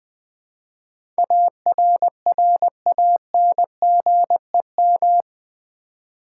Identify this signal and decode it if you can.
Morse code, heard as a keyed tone, about 20 wpm, 700 Hz. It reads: ARRANGEM
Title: Build A Word - 4 Letters At A Time - 11-Letter Words - 20wpm